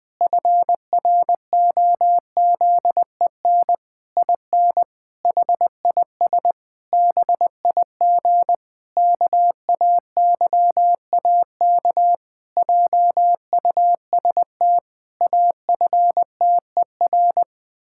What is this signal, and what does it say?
Morse code, heard as a keyed tone, about 20 wpm, 700 Hz. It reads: FROZEN IN HIS BIG KAYAK JUST AFTER
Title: Pangram Challenge v2 - 20wpm